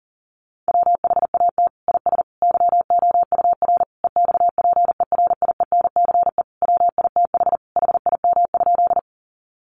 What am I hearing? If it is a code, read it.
Morse code, heard as a keyed tone, about 40 wpm, 700 Hz. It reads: WHAT IS YOUR EXPERIENCE WITH HIM?
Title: Sentences from Top 600 Words 40wpm